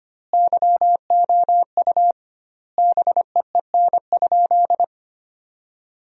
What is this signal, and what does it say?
Morse code, heard as a keyed tone, about 25 wpm, 700 Hz. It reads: YOU BEEN?
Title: 2x Word-Spacing - Sentences from Top 400 Words - 25wpm